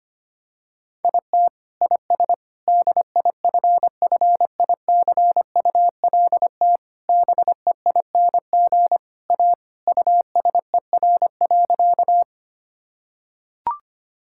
Morse code, heard as a keyed tone, about 25 wpm, 700 Hz. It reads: IT IS DIFFICULT BEING A USER.
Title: Sentences from Top 1000 Words - Review Part 5 - 25wpm